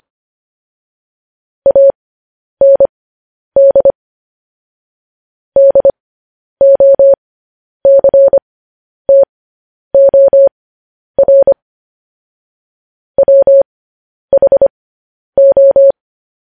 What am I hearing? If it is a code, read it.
Morse code, heard as a keyed tone, about 25 wpm, spaced as for 10 wpm, 550 Hz. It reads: AND DOCTOR WHO
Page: news headlines in morse code at 10 wpm